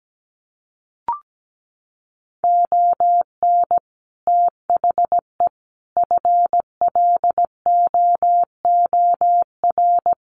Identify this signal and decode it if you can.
Morse code, heard as a keyed tone, about 17 wpm, 700 Hz. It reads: ON THE FLOOR
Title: Sets of 5 Words 17wpm